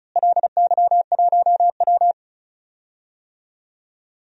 Morse code, heard as a keyed tone, about 36 wpm, 700 Hz. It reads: LY1W
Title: DXCC-100 Call Signs - 4 Characters - Repeated 3x - 35wpm